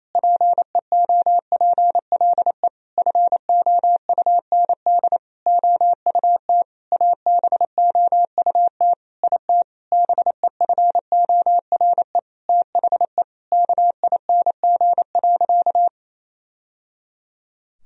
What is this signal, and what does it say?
Morse code, heard as a keyed tone, about 28 wpm, 700 Hz. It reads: PEOPLE FOUND OUT ABOUT IT BEFORE THE KING.
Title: Sentences from Top 1000 Words - Review Part 3 - 28wpm